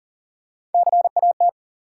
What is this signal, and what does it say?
Morse code, heard as a keyed tone, about 40 wpm, 700 Hz. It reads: CAT
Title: Three Letter Words - 40wpm